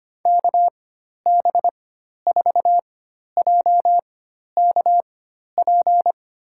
Morse code, heard as a keyed tone, about 25 wpm, spaced as for 12 wpm, 700 Hz. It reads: KB4JKP
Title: USA Call Signs - 4x Character Spacing - 25wpm